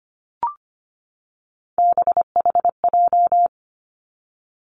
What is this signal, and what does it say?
Morse code, heard as a keyed tone, about 25 wpm, 700 Hz. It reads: BHJ
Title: Three Challenging Letter-Number Combinations 25wpm